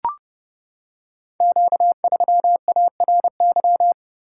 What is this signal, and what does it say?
Morse code, heard as a keyed tone, about 30 wpm, 700 Hz. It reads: Q3ARY
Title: Code Groups - 5 Letter:Numbers - 30wpm